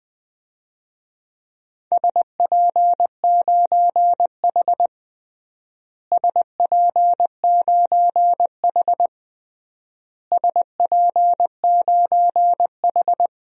Morse code, heard as a keyed tone, about 20 wpm, 690 Hz. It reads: SP9H SP9H SP9H
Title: DXCC-100 Call Signs - 4 Characters - Repeated 3x - 20wpm